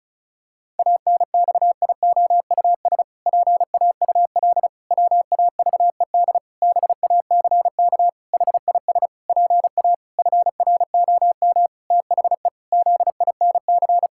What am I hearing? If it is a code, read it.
Morse code, heard as a keyed tone, about 36 wpm, 700 Hz. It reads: ANXIOUS PAUL WAVED BACK HIS PA FROM THE ZINC
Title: Pangram Challenge v2 - 35wpm